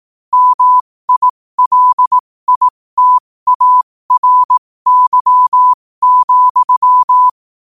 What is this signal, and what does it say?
Morse code, heard as a keyed tone, about 18 wpm, spaced as for 15 wpm, 1000 Hz. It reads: MILITARY,
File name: qotdfiles / 15wpm